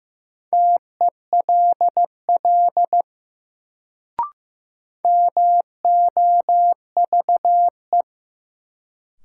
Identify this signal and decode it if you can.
Morse code, heard as a keyed tone, about 15 wpm, 700 Hz. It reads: TELL MOVE
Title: Sentences from Top 200 Words - New Word Drill - 15wpm